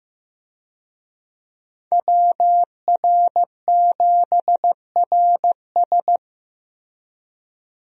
Morse code, heard as a keyed tone, about 15 wpm, 700 Hz. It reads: WR7RS
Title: USA Call Signs - 5 Characters - Repeated 3x - 15wpm